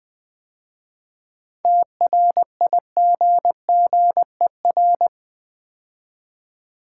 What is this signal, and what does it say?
Morse code, heard as a keyed tone, about 20 wpm, 700 Hz. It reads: TRIGGER
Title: 7 Letter Words - First 3 Letters Given - 20wpm